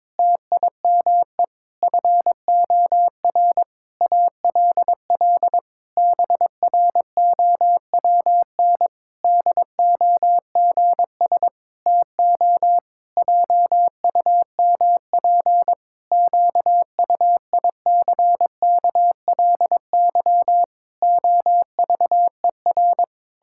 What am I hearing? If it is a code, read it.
Morse code, heard as a keyed tone, about 22 wpm, 700 Hz. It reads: TIME FOR ALL BROWN DOGS TO JUMP QUICKLY OVER
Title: Pangram Challenge v2 - 22wpm